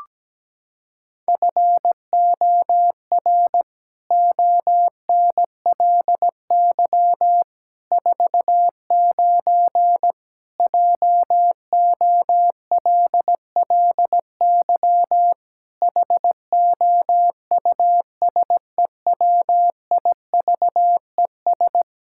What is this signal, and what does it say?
Morse code, heard as a keyed tone, about 17 wpm, 700 Hz. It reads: FOR ONLY 49 JOLLY HOUSEWIVES
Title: Pangram Challenge v2 - 17wpm